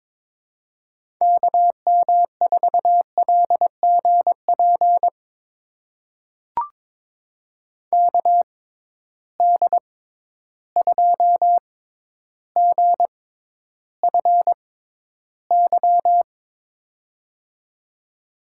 Morse code, heard as a keyed tone, about 22 wpm, spaced as for 11 wpm, 700 Hz. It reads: KM4LGP KD2GFY
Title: USA Call Signs - 6 Characters - 6x Character Spacing - 22wpm